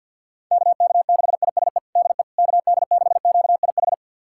Morse code, heard as a keyed tone, about 52 wpm, 700 Hz. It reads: KK6ISE DE KD6QIH
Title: USA Callsign DE Callsign - 6 Characters - 50wpm